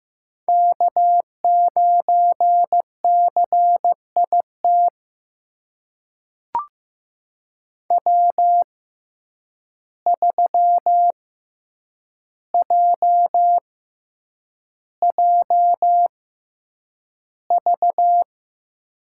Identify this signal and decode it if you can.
Morse code, heard as a keyed tone, about 15 wpm, spaced as for 8 wpm, 700 Hz. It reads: K9CIT W3JJV
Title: USA Call Signs - 5 Characters - 6x Character Spacing - 15wpm